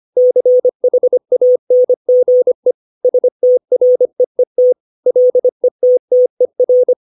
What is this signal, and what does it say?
Morse code, heard as a keyed tone, about 25 wpm, 500 Hz. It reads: CHANGE STREET LETTER
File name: 632 words / 632 Words 25WPM